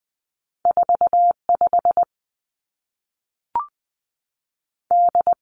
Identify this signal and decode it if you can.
Morse code, heard as a keyed tone, about 20 wpm, 700 Hz. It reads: 4 5 D
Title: Two Challenging Letter-Number Combinations 20wpm